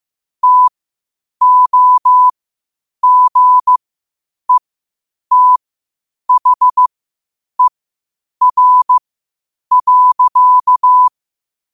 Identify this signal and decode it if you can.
Morse code, heard as a keyed tone, about 15 wpm, spaced as for 8 wpm, 1000 Hz. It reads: TOGETHER.
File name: qotdfiles / 8wpm